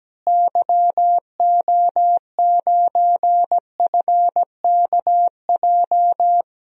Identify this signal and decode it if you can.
Morse code, heard as a keyed tone, about 17 wpm, 700 Hz. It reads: YO9FKJ
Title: DXCC-100 Call Signs - 6 Characters - Repeated 2x - 17wpm